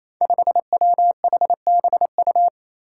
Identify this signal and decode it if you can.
Morse code, heard as a keyed tone, about 28 wpm, 700 Hz. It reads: 5WHBU
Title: Five Challenging Letter-Number Combinations 28wpm